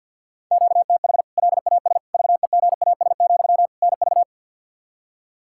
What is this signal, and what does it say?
Morse code, heard as a keyed tone, about 47 wpm, 700 Hz. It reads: QTH LAS VEGAS, NV
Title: K1USN SST QSO Practice 50wpm